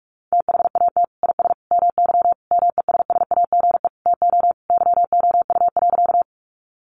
Morse code, heard as a keyed tone, about 44 wpm, 700 Hz. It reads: THAT IS MY MESSAGE TO YOU.
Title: Sentences from Top 1100 Words 45wpm